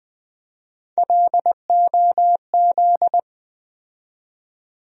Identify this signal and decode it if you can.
Morse code, heard as a keyed tone, about 20 wpm, 700 Hz. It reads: LOZ
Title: USA Call Sign Postfixes - 3 Letters 20wpm